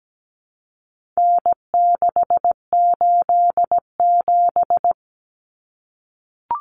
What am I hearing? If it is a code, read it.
Morse code, heard as a keyed tone, about 17 wpm, 700 Hz. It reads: N687
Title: Common Cut Numbers - 4-Digit 17wpm